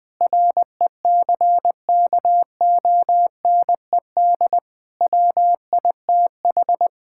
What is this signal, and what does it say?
Morse code, heard as a keyed tone, about 20 wpm, 700 Hz. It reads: RECKONED WITH
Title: Sets of 6 Words - 20wpm